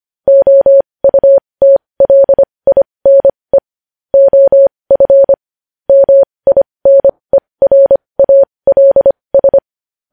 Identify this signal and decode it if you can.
Morse code, heard as a keyed tone, about 25 wpm, spaced as for 20 wpm, 550 Hz. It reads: OUTLINE OF MINERALS
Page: news-headlines-in-morse-code-at-20-wpm